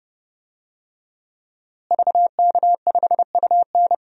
Morse code, heard as a keyed tone, about 30 wpm, 700 Hz. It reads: VK5UN
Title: DXCC-100 Call Signs - 5 Characters - Repeated 3x - 30wpm